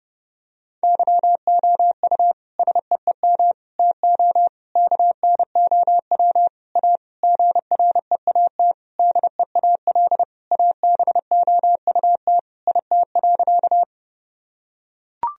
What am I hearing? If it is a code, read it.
Morse code, heard as a keyed tone, about 30 wpm, 700 Hz. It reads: YOU SEEM TO KNOW A GREAT DEAL ABOUT IT.
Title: Sentences from Top 1000 Words - Review Part 1 - 30wpm